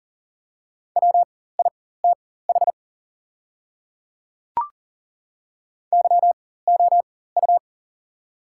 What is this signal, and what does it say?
Morse code, heard as a keyed tone, about 40 wpm, spaced as for 19 wpm, 700 Hz. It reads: WITH YOU
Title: 4x Character Spacing - Top 200 Words - 40wpm